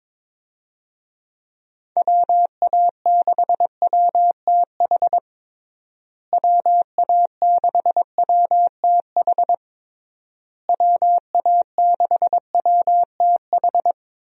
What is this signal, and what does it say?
Morse code, heard as a keyed tone, about 22 wpm, 700 Hz. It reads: WA6WTH WA6WTH WA6WTH
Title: USA Call Signs - 6 Characters - Repeated 3x - 22wpm